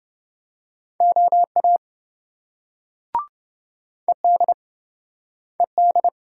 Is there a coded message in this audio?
Yes: Morse code from 1.0 to 6.1 s, heard as a keyed tone, about 27 wpm, 700 Hz. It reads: OA ED ED